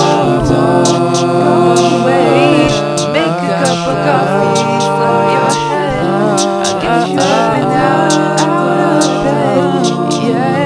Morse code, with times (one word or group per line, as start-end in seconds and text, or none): none